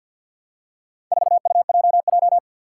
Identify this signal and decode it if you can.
Morse code, heard as a keyed tone, about 52 wpm, 700 Hz. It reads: 4UJJ
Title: Four Challenging Letter-Number Combinations 50wpm